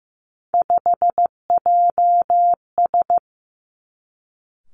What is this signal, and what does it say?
Morse code, heard as a keyed tone, about 15 wpm, 700 Hz. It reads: 5JS
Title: Three Challenging Letter-Number Combinations 15wpm